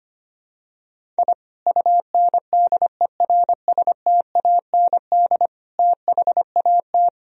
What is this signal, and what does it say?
Morse code, heard as a keyed tone, about 25 wpm, 700 Hz. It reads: I UNDERSTAND THAT
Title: Sentences from Top 1000 Words - Review Part 1 - 25wpm